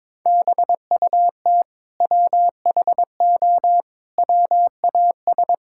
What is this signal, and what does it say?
Morse code, heard as a keyed tone, about 22 wpm, 700 Hz. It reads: BUT WHO WAS
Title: Sentences from Top 1000 Words - Review Part 5 - 22wpm